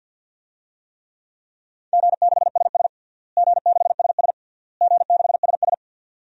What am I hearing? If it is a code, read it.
Morse code, heard as a keyed tone, about 47 wpm, 700 Hz. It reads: G6SS G6SS G6SS
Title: DXCC-100 Call Signs - 4 Characters - Repeated 3x - 50wpm